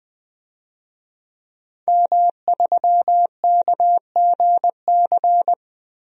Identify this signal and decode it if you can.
Morse code, heard as a keyed tone, about 21 wpm, 700 Hz. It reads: M3KGC
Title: DXCC-100 Call Signs - 5 Characters - 20wpm